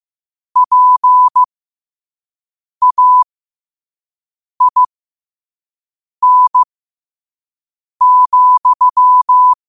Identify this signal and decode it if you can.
Morse code, heard as a keyed tone, about 15 wpm, spaced as for 5 wpm, 1000 Hz. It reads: PAIN,
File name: qotdfiles / 5wpm